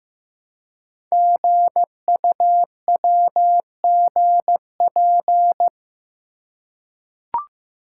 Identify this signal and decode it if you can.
Morse code, heard as a keyed tone, about 15 wpm, 700 Hz. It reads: GUWGP
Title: Code Groups - 5 Letters - 15wpm